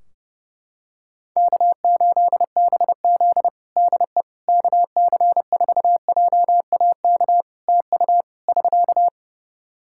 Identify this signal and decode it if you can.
Morse code, heard as a keyed tone, about 32 wpm, 700 Hz. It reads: K8BZ DE KC4JAK TU <SK>
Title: Practice QSO - Exchange 4 - Ending - 30wpm